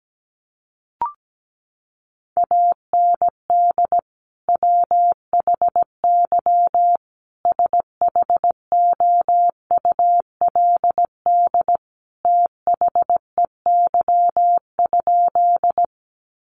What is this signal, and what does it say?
Morse code, heard as a keyed tone, about 17 wpm, 700 Hz. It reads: AND WHY SHOULD THEY?
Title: Sentences from Top 300 Words 17wpm